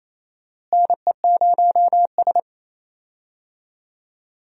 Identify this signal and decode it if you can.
Morse code, heard as a keyed tone, about 27 wpm, 700 Hz. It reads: NE0S
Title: USA Call Signs - 4 Characters - Repeated 2x - 28wpm